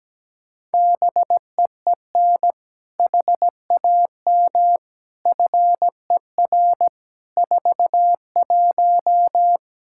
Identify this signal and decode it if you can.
Morse code, heard as a keyed tone, about 17 wpm, 700 Hz. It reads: BEEN HAM FER 41
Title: Practice QSO - Exchange 3 - More Info - 17wpm